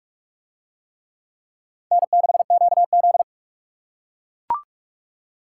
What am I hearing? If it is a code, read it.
Morse code, heard as a keyed tone, about 38 wpm, 700 Hz. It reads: N6QZ